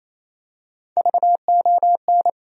Morse code, heard as a keyed tone, about 28 wpm, 700 Hz. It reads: VON
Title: USA Call Sign Postfixes - 3 Letters 28wpm